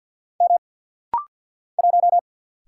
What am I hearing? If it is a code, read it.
Morse code, heard as a keyed tone, about 50 wpm, 700 Hz. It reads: M 1
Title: Single Letter-Number - Rapid-Fire 50wpm